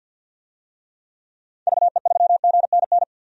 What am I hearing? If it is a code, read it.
Morse code, heard as a keyed tone, about 51 wpm, 700 Hz. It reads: VE3GNN
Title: DXCC-100 Call Signs - 6 Characters - 50wpm